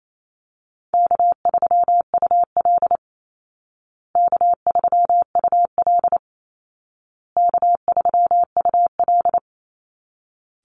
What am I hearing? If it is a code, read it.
Morse code, heard as a keyed tone, about 28 wpm, 700 Hz. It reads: K3UL K3UL K3UL
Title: Easy Call Signs Repeated 3x - 28wpm